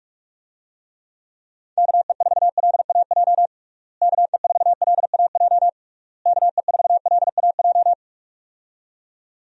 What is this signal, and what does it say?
Morse code, heard as a keyed tone, about 46 wpm, 700 Hz. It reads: KE4LAJ KE4LAJ KE4LAJ
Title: USA Call Signs - 6 Characters - Repeated 3x - 45wpm